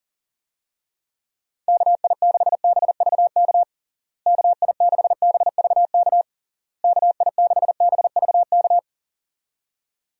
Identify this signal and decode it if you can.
Morse code, heard as a keyed tone, about 40 wpm, 700 Hz. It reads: KI6BVK KI6BVK KI6BVK
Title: USA Call Signs - 6 Characters - Repeated 3x - 40wpm